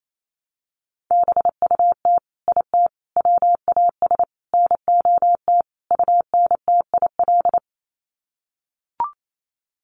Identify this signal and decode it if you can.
Morse code, heard as a keyed tone, about 28 wpm, 700 Hz. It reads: BUT IT WAS NOT UNTIL
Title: Sets of 5 Words Encore2 28wpm